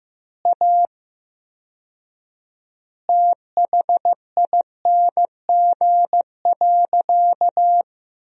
Morse code, heard as a keyed tone, about 15 wpm, 700 Hz. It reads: A THING.
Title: 4x Word-Spacing - Sentences from Top 200 Words - 15wpm